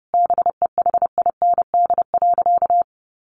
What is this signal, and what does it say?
Morse code, heard as a keyed tone, about 30 wpm, 700 Hz. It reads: BEHIND.
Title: Sentences from Top 700 Words 30wpm